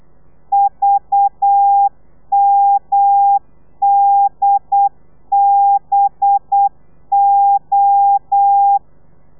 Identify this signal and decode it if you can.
Morse code, heard as a keyed tone, about 8 wpm, 800 Hz. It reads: VMDBO
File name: telegrafia / i2tfj / Lez4 35